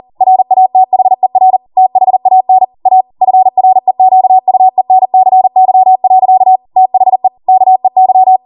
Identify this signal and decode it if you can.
Morse code, heard as a keyed tone, about 40 wpm, 750 Hz. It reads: RATHER THAN A FREQUENCY. THE KEY